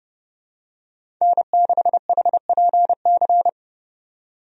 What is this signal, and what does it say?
Morse code, heard as a keyed tone, about 32 wpm, 700 Hz. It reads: N6HPC
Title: USA Call Signs - 5 Characters - 30wpm